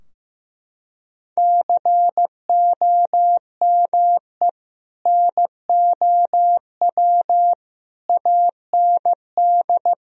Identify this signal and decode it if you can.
Morse code, heard as a keyed tone, about 15 wpm, 700 Hz. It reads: COME NOW AND
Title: Sentences from Top 1000 Words - Review Part 4 - 15wpm